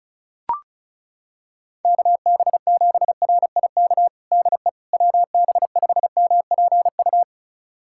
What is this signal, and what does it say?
Morse code, heard as a keyed tone, about 36 wpm, 700 Hz. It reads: KB7RIK DE WB5MPU
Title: USA Callsign DE Callsign - 6 Characters - 35wpm